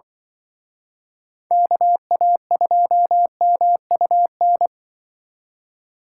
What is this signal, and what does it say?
Morse code, heard as a keyed tone, about 24 wpm, 700 Hz. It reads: KA2MUN
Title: Speed Racing - USA 2x3 Call Signs 20wpm